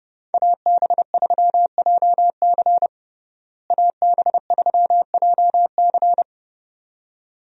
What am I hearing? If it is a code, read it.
Morse code, heard as a keyed tone, about 30 wpm, 700 Hz. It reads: AB3JC AB3JC
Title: USA Call Signs - 5 Characters - Repeated 2x - 30wpm